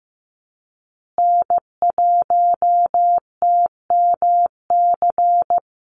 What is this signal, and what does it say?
Morse code, heard as a keyed tone, about 15 wpm, 700 Hz. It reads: N1TMC
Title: USA Call Signs Repeated 2x 15wpm